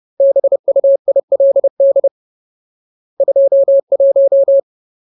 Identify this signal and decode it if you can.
Morse code, heard as a keyed tone, about 30 wpm, 550 Hz. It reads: BUILD 21
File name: Ultimate Word List / 30WPM Ultimate